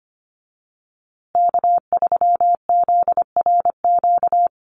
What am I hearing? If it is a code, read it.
Morse code, heard as a keyed tone, about 25 wpm, 700 Hz. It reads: K3ZRQ
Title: USA Call Signs Repeated 3x 25wpm